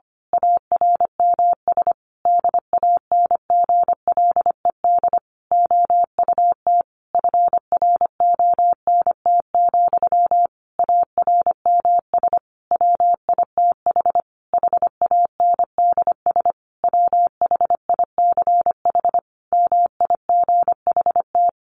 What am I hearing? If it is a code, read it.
Morse code, heard as a keyed tone, about 25 wpm, 700 Hz. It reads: ARMS DANGLED OUT FRONT, ARMS WITH HANDS WHICH MIGHT
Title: A Sound Of Thunder 25wpm